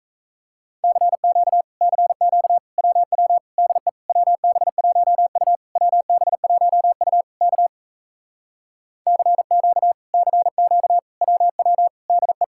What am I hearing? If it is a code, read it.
Morse code, heard as a keyed tone, about 45 wpm, 700 Hz. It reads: CQ CQ WW DE WB1U WB1U K CQ CQ WW DE